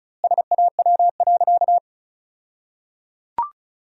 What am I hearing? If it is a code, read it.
Morse code, heard as a keyed tone, about 36 wpm, 700 Hz. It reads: SAW.
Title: 2x Word-Spacing - Sentences from Top 300 Words - 35wpm